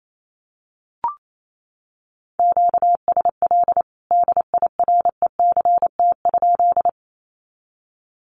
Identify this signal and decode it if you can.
Morse code, heard as a keyed tone, about 28 wpm, 700 Hz. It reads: QSL DIRECT?
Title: Common QSO Phrases 28wpm